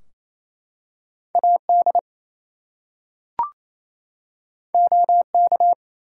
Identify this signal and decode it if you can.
Morse code, heard as a keyed tone, about 27 wpm, 700 Hz. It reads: AD OK
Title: Two Letter Words - 28wpm